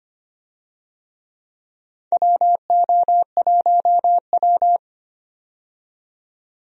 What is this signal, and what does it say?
Morse code, heard as a keyed tone, about 25 wpm, 700 Hz. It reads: WO1W
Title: USA Call Signs - 4 Characters - 25wpm